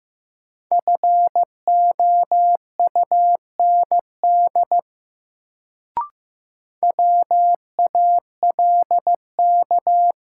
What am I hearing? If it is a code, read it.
Morse code, heard as a keyed tone, about 15 wpm, 700 Hz. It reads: FOUND WALK
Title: Sentences from Top 300 Words - New Word Drill - 15wpm